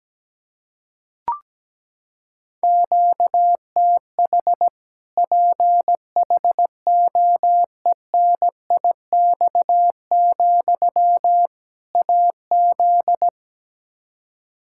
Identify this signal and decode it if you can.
Morse code, heard as a keyed tone, about 17 wpm, 700 Hz. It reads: QTH PHOENIX, AZ
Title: Practice QSO - Exchange 1 - Beginning - 17wpm